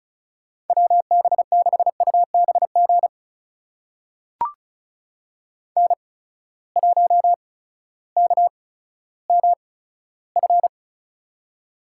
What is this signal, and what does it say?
Morse code, heard as a keyed tone, about 34 wpm, spaced as for 16 wpm, 700 Hz. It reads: WB6UBG N1KMF
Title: USA Call Signs - 8x Character Spacing - 35wpm